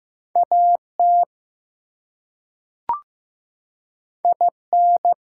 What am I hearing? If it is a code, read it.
Morse code, heard as a keyed tone, about 15 wpm, 700 Hz. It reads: AT IN
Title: Two Letter Words - 15wpm